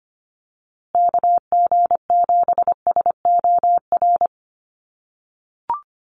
Morse code, heard as a keyed tone, about 25 wpm, 700 Hz. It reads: KG7SOR
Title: USA Call Signs Repeated 2x 25wpm